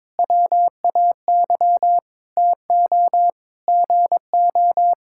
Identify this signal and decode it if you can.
Morse code, heard as a keyed tone, about 22 wpm, 700 Hz. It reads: WAY TO GO
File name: Sentences with 9-Word Phrases - 22wpm